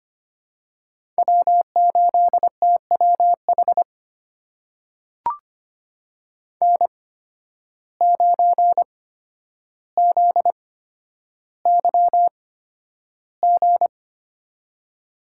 Morse code, heard as a keyed tone, about 25 wpm, spaced as for 11 wpm, 700 Hz. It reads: W8TWH N9ZYG
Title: USA Call Signs - 5 Characters - 8x Character Spacing - 25wpm